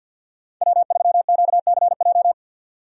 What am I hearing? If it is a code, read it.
Morse code, heard as a keyed tone, about 52 wpm, 700 Hz. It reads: W3QCJ
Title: USA Call Signs - 5 Characters - 50wpm